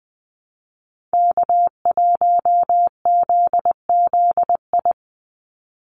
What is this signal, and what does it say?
Morse code, heard as a keyed tone, about 20 wpm, 700 Hz. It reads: K1ZZI
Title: Call Signs - 20wpm